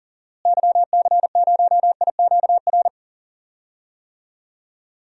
Morse code, heard as a keyed tone, about 40 wpm, 700 Hz. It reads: YC0IQR
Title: DXCC-100 Call Signs - 6 Characters - Repeated 2x - 40wpm